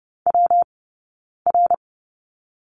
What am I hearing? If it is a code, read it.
Morse code, heard as a keyed tone, about 30 wpm, spaced as for 9 wpm, 690 Hz. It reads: WR
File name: Lesson 19d - Letter w - Two Characters 30wpm